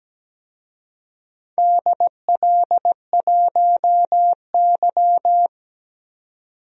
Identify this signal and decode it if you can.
Morse code, heard as a keyed tone, about 17 wpm, 700 Hz. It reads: DL1Y
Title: DXCC-100 Call Signs - 4 Characters - 17wpm